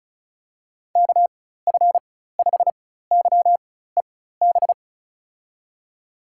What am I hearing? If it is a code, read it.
Morse code, heard as a keyed tone, about 34 wpm, spaced as for 16 wpm, 700 Hz. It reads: KF5YEB
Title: USA Call Signs - 6 Characters - 4x Character Spacing - 35wpm